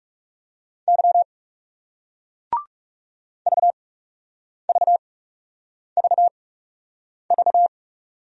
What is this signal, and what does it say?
Morse code, heard as a keyed tone, about 40 wpm, spaced as for 8 wpm, 700 Hz. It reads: Y VVVV